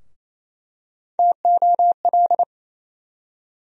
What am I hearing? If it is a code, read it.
Morse code, heard as a keyed tone, about 28 wpm, 700 Hz. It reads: TOL
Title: USA Call Sign Postfixes - 3 Letters 28wpm